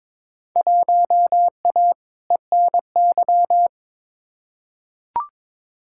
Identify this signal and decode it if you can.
Morse code, heard as a keyed tone, about 22 wpm, 700 Hz. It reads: 1A ENY
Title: Field Day Exchanges - 22wpm